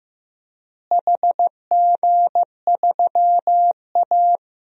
Morse code, heard as a keyed tone, about 15 wpm, 700 Hz. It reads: HG3A
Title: DXCC-100 Call Signs - 4 Characters - Repeated 2x - 15wpm